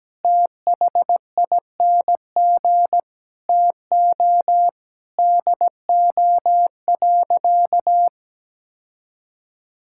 Morse code, heard as a keyed tone, about 17 wpm, 700 Hz. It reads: THING TO DO.
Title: Sentences from Top 1000 Words - Review Part 5 - 17wpm